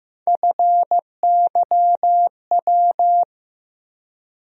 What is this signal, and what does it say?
Morse code, heard as a keyed tone, about 15 wpm, 700 Hz. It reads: FYW
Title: USA Call Sign Postfixes - 3 Letters 15wpm